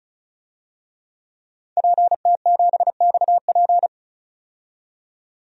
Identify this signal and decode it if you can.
Morse code, heard as a keyed tone, about 36 wpm, 700 Hz. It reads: PT7XP
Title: DXCC-100 Call Signs - 5 Characters - 35wpm